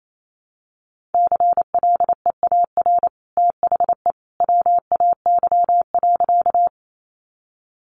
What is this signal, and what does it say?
Morse code, heard as a keyed tone, about 28 wpm, 700 Hz. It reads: CLEAR THE WAY.
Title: Sentences from Top 500 Words 28wpm